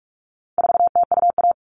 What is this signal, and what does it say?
Morse code, heard as a keyed tone, about 44 wpm, 700 Hz. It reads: 4TUA
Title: All Cut Numbers - 4-Digit 45wpm